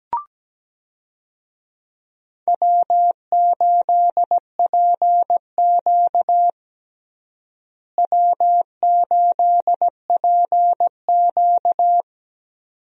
Morse code, heard as a keyed tone, about 17 wpm, 700 Hz. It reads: W8PQ W8PQ
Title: USA Call Signs - 4 Characters - Repeated 2x - 17wpm